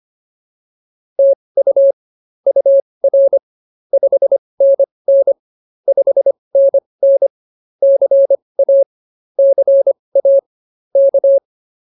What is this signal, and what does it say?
Morse code, heard as a keyed tone, about 24 wpm, spaced as for 20 wpm, 550 Hz. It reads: TU UR 5NN 5NN CA CA K